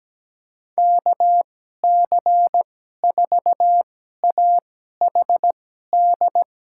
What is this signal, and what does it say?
Morse code, heard as a keyed tone, about 17 wpm, spaced as for 12 wpm, 700 Hz. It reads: KC4AHD
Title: USA Call Signs - 6 Characters - 2x Character Spacing - 17wpm